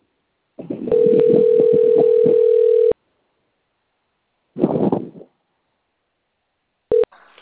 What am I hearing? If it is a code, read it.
Ringback tone (North America), 440 + 480 Hz.